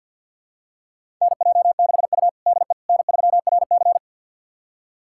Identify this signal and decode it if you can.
Morse code, heard as a keyed tone, about 52 wpm, 700 Hz. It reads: NJ6U DE N3RC
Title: USA Callsign DE Callsign - 4 Characters - 50wpm